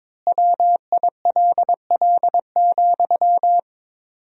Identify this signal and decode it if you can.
Morse code, heard as a keyed tone, about 22 wpm, 700 Hz. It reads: WILL,
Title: 4x Word-Spacing - Sentences from Top 200 Words - 22wpm